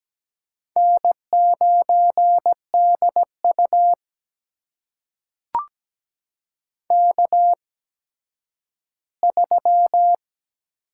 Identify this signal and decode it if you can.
Morse code, heard as a keyed tone, about 17 wpm, spaced as for 10 wpm, 700 Hz. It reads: N9DU K3
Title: USA Call Signs - 4 Characters - 8x Character Spacing - 17wpm